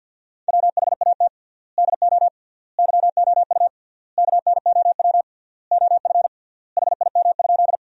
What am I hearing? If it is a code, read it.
Morse code, heard as a keyed tone, about 50 wpm, 700 Hz. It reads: WHAT DO YOU KNOW OF HIM?
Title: 3x Word-Spacing - Sentences from Top 100 Words - 50wpm